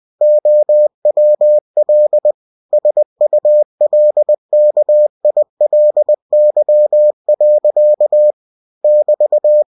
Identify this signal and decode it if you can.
Morse code, heard as a keyed tone, about 20 wpm, 600 Hz. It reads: OWL SULKILY. <BT>
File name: cw-books / winnie-the-pooh / winnie-20wpm-page-0105